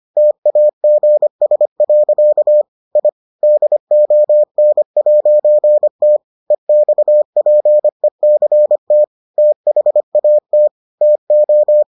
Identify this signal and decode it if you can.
Morse code, heard as a keyed tone, about 25 wpm, 600 Hz. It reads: TAGS. I DON'T EXPECT THAT TO